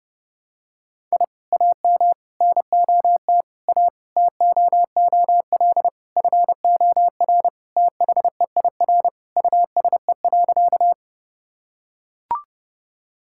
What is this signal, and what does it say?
Morse code, heard as a keyed tone, about 32 wpm, 700 Hz. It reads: I AM NOT A TOOL FOR THEIR USE.
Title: Sentences from Top 1000 Words - Review Part 6 - 30wpm